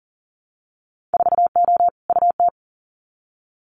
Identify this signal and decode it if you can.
Morse code, heard as a keyed tone, about 38 wpm, 700 Hz. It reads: 4O UT
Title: Winter Field Day Exchanges Encore 40wpm